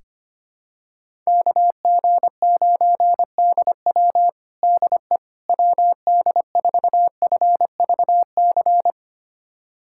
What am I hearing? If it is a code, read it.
Morse code, heard as a keyed tone, about 25 wpm, 700 Hz. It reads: KG9DW DE WD4FVC
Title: USA Callsign DE Callsign 25wpm